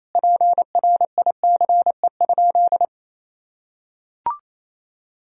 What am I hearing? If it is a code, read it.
Morse code, heard as a keyed tone, about 28 wpm, 700 Hz. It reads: PRICE?
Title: Sentences from Top 1000 Words - Review Part 4 - 28wpm